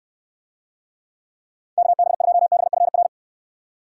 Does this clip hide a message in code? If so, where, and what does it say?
Yes: Morse code from 1.8 to 3.1 s, heard as a keyed tone, about 69 wpm, 700 Hz. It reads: KB2BFR